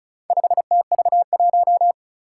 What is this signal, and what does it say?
Morse code, heard as a keyed tone, about 36 wpm, 700 Hz. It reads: 5TV1
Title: Four Challenging Letter-Number Combinations 35wpm